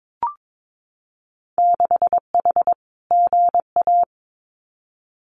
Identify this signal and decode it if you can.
Morse code, heard as a keyed tone, about 22 wpm, 700 Hz. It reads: 6H GA
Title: Winter Field Day Exchanges Encore 22wpm